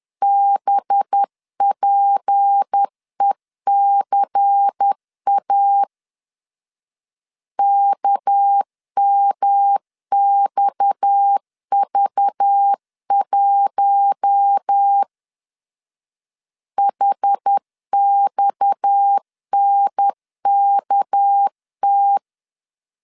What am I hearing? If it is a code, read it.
Morse code, heard as a keyed tone, about 10 wpm, 800 Hz. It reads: BPECA KMXV1 HXNKT
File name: LEZIONI / 6 PKX0 / 6 PKX 10wpm 2426